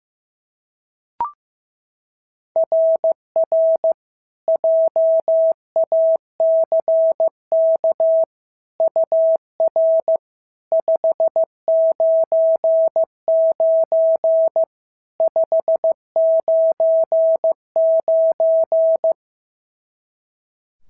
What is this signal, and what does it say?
Morse code, heard as a keyed tone, about 15 wpm, 650 Hz. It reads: RR JACK UR 599 599
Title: SOTA QSO Practice 15wpm